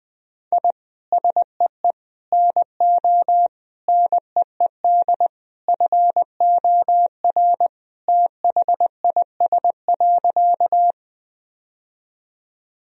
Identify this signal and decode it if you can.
Morse code, heard as a keyed tone, about 20 wpm, 700 Hz. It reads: I SEE NO NEED FOR THIS.
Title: Sentences from Top 1000 Words - Review Part 1 - 20wpm